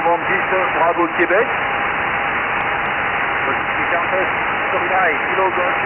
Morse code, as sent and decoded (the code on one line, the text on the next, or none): none